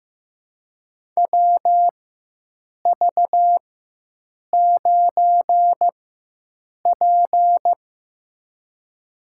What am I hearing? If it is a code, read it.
Morse code, heard as a keyed tone, about 15 wpm, spaced as for 7 wpm, 700 Hz. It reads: WV9P
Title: USA Call Signs - 4 Characters - 4x Character Spacing - 15wpm